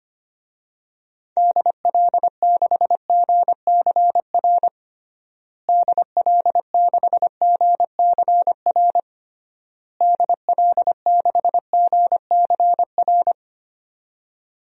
Morse code, heard as a keyed tone, about 25 wpm, 700 Hz. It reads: DL6GCR DL6GCR DL6GCR
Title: DXCC-100 Call Signs - 6 Characters - Repeated 3x - 25wpm